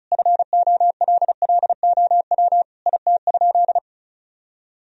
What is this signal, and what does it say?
Morse code, heard as a keyed tone, about 36 wpm, 700 Hz. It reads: FOLLOW IT?
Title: Sentences from Top 1000 Words - Review Part 1 - 35wpm